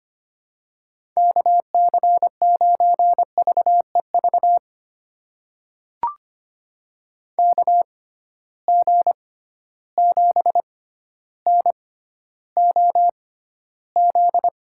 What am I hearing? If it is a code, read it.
Morse code, heard as a keyed tone, about 25 wpm, spaced as for 13 wpm, 700 Hz. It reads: KC9VEV KG7NOZ
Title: USA Call Signs - 6 Characters - 6x Character Spacing - 25wpm